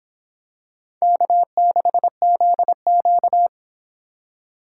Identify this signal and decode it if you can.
Morse code, heard as a keyed tone, about 26 wpm, 700 Hz. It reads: K6ZQ